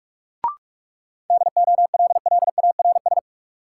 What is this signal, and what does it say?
Morse code, heard as a keyed tone, about 46 wpm, 700 Hz. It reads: DOLLARS
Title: Sets of 5 Words - New Word Drill - 45wpm